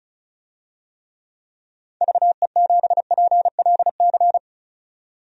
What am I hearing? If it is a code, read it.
Morse code, heard as a keyed tone, about 34 wpm, 700 Hz. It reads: VE7PLC
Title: DXCC-100 Call Signs - 6 Characters - 35wpm